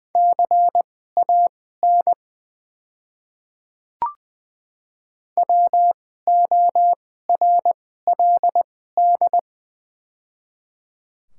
Morse code, heard as a keyed tone, about 20 wpm, spaced as for 14 wpm, 700 Hz. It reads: CAN WORLD
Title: 2x Character Spacing - Top 300 Words - 20wpm